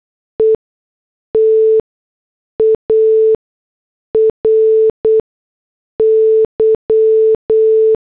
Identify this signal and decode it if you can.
Morse code, heard as a keyed tone, about 8 wpm, spaced as for 6 wpm, 440 Hz. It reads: ETARY
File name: morse / headlines